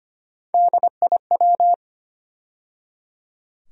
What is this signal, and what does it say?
Morse code, heard as a keyed tone, about 25 wpm, 700 Hz. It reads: DIW
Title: USA Call Sign Postfixes - 3 Letters 25wpm